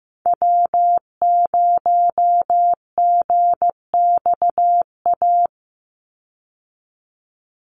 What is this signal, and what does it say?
Morse code, heard as a keyed tone, about 15 wpm, 700 Hz. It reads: W0GXA
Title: Call Signs Repeated 2x - 15wpm